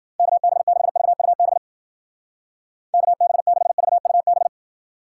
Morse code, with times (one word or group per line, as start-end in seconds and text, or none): 0.2-1.6 KB6VUB
2.9-4.5 KB6VUB